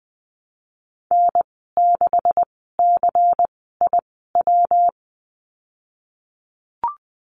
Morse code, heard as a keyed tone, about 19 wpm, spaced as for 14 wpm, 700 Hz. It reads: N6CIW
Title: 161 Lesson 26g - Letter g - Call Signs 20wpm